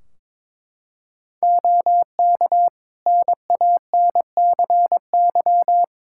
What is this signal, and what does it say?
Morse code, heard as a keyed tone, about 22 wpm, 700 Hz. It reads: OK NANCY
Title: Practice QSO - Exchange 2 - Info - 22wpm